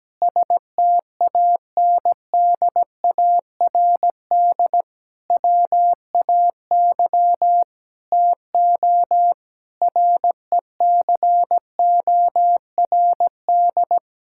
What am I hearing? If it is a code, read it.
Morse code, heard as a keyed tone, about 17 wpm, 700 Hz. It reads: STANDARD WAY TO RECORD
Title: Sentences from Top 1000 Words - Review Part 6 - 17wpm